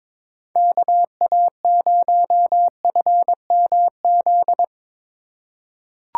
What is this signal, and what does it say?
Morse code, heard as a keyed tone, about 21 wpm, 700 Hz. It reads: KA0FMZ
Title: USA Call Signs - 6 Characters - 6x Character Spacing - 22wpm